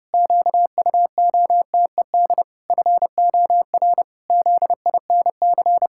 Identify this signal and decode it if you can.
Morse code, heard as a keyed tone, about 30 wpm, 700 Hz. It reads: QUOTED FOR ZINC
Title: Pangram Challenge v2 - 30wpm